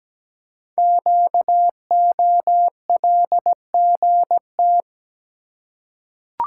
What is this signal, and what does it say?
Morse code, heard as a keyed tone, about 17 wpm, 700 Hz. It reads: QOLGT
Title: Code Groups - 5 Letters - 17wpm